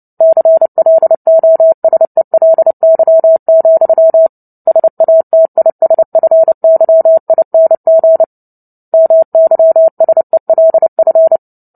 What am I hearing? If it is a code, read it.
Morse code, heard as a keyed tone, about 30 wpm, 650 Hz. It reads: CLOSELY, SATISFYING MYSELF
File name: Book Two / 03-Chapter 3 - 29 WPM